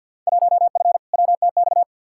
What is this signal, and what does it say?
Morse code, heard as a keyed tone, about 47 wpm, 700 Hz. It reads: 1F WTX
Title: Field Day Exchanges - 50wpm